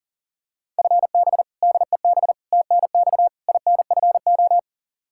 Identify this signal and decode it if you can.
Morse code, heard as a keyed tone, about 37 wpm, 700 Hz. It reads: FB DEB TNX INFO
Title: Practice QSO - Exchange 3 - More Info - 40wpm